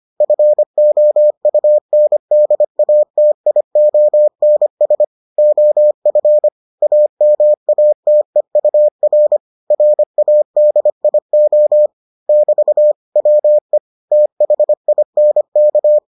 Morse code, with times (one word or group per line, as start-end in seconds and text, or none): 0.2-5.0 FOUNDATIONS
5.4-6.5 OF
6.8-9.4 AMATEUR
9.7-11.9 RADIO
12.3-12.9 <BT>
13.2-13.8 WE
14.1-16.0 THINK